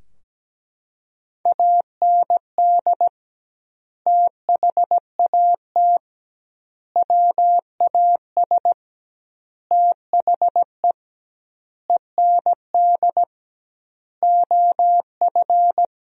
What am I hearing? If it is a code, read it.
Morse code, heard as a keyed tone, about 17 wpm, 700 Hz. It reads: AND THAT WAS THE END OF
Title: 2x Word-Spacing - Sentences from Top 200 Words - 17wpm